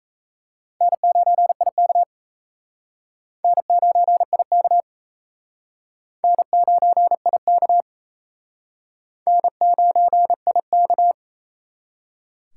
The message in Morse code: N9IK N9IK N9IK N9IK

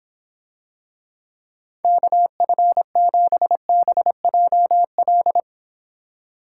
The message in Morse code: KF7BJL

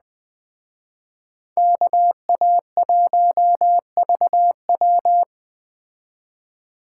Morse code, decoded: KA1VW

